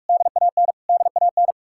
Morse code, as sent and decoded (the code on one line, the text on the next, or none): -.. .- -. / -.. .- -.
DAN DAN